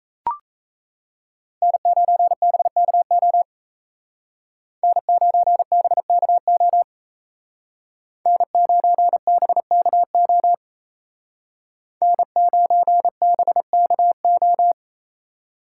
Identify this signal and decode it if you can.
Morse code, heard as a keyed tone, about 35 wpm, 700 Hz. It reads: N9BKO N9BKO N9BKO N9BKO